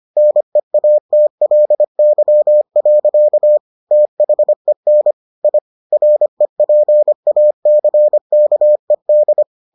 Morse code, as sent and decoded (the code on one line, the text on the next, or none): -. .- - .-.. -.-- .-.-.- / - .... -. / .. / .-. .--. .- -.-. -.- . -..
NEATLY. THEN I REPACKED